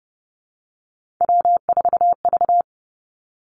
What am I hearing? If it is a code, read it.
Morse code, heard as a keyed tone, about 30 wpm, 700 Hz. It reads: W4V
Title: Three Challenging Letter-Number Combinations 30wpm